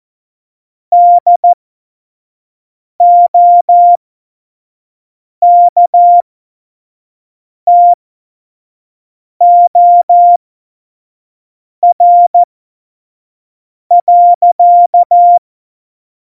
Morse code, heard as a keyed tone, about 14 wpm, spaced as for 5 wpm, 700 Hz. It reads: DOKTOR.
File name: Morsekurs / Witze / Witz 23. Januar 2018, 25 F70